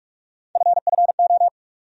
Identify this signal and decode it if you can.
Morse code, heard as a keyed tone, about 43 wpm, 700 Hz. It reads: FFO
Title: USA Call Sign Postfixes - 3 Letters 45wpm